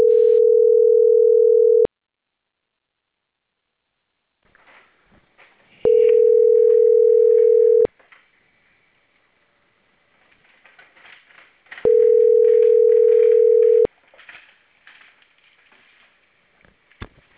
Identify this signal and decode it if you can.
Ringback tone (North America), 440 + 480 Hz.